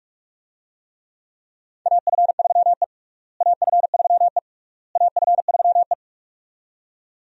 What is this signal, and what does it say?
Morse code, heard as a keyed tone, about 46 wpm, 700 Hz. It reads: AF3E AF3E AF3E
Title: USA Call Signs - 4 Characters - Repeated 3x - 45wpm